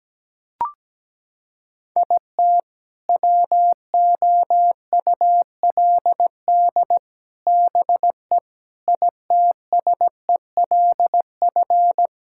Morse code, heard as a keyed tone, about 17 wpm, 700 Hz. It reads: IT WOULD BE ITSELF
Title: Sentences from Top 1000 Words - Review Part 5 - 17wpm